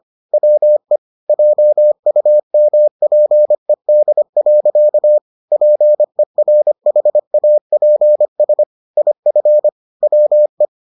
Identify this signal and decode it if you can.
Morse code, heard as a keyed tone, about 25 wpm, 600 Hz. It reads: WE JUMPED. PERHAPS IF WE